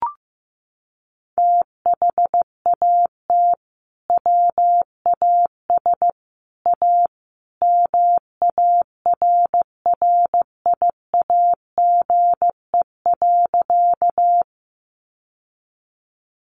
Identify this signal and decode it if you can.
Morse code, heard as a keyed tone, about 15 wpm, 700 Hz. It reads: THAT WAS A MARRIAGE.
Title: Sentences from Top 800 Words 15wpm